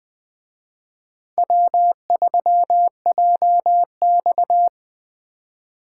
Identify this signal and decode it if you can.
Morse code, heard as a keyed tone, about 20 wpm, 700 Hz. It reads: W3JX